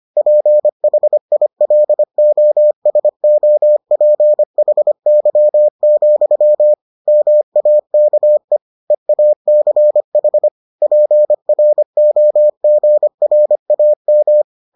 Morse code, heard as a keyed tone, about 25 wpm, 600 Hz. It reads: PHILOSOPHY, MAKE EACH PROGRAM